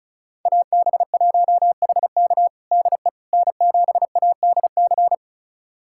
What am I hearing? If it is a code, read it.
Morse code, heard as a keyed tone, about 36 wpm, 700 Hz. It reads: AB1HK DE N7ADC
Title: USA Callsign DE Callsign - 5 Characters - 35wpm